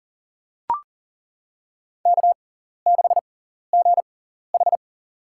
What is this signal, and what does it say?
Morse code, heard as a keyed tone, about 39 wpm, spaced as for 14 wpm, 700 Hz. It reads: K6GH
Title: USA Call Signs - 6x Character Spacing - 40wpm